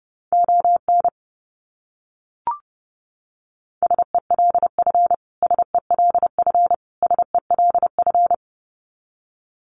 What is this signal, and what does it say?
Morse code, heard as a keyed tone, about 28 wpm, 700 Hz. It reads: ON SELF SELF SELF
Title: Top 300 Words Repeated 30wpm